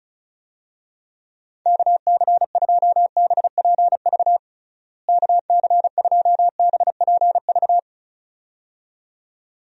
Morse code, heard as a keyed tone, about 34 wpm, 700 Hz. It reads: KC2BPV KC2BPV
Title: USA Call Signs - 6 Characters - Repeated 2x - 35wpm